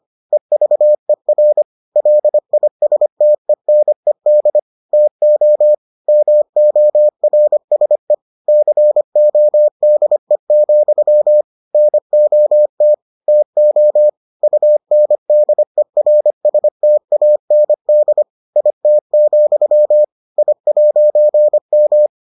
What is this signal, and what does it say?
Morse code, heard as a keyed tone, about 25 wpm, 600 Hz. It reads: EVER LISTENED TO MORSE CODE, NOT TO UNDERSTAND IT, I'M